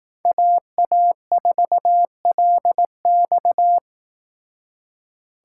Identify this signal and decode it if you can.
Morse code, heard as a keyed tone, about 18 wpm, 700 Hz. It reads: AA4LX
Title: Speed Racing - USA 2x2 Call Signs 15wpm